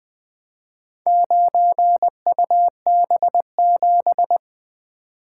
Morse code, heard as a keyed tone, about 20 wpm, 700 Hz. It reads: 9UB7
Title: Four Letter-Number Combinations - 20wpm